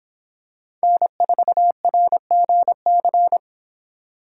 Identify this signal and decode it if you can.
Morse code, heard as a keyed tone, about 24 wpm, 700 Hz. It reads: N4RGC